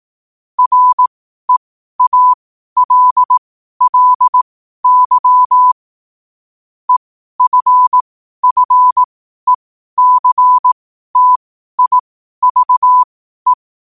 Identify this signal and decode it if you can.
Morse code, heard as a keyed tone, about 18 wpm, spaced as for 12 wpm, 1000 Hz. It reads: REALLY EFFECTIVE
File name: qotdfiles / 12wpm